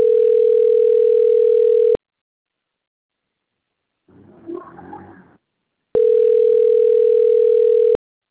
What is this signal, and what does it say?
Ringback tone (North America), 440 + 480 Hz.